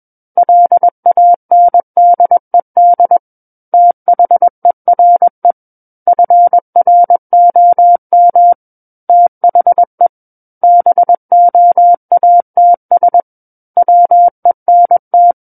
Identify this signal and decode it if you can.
Morse code, heard as a keyed tone, about 21 wpm, 700 Hz. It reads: LANDED THERE FROM THE BOATS WENT
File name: Book One / 12-Chapter 12 - 21 WPM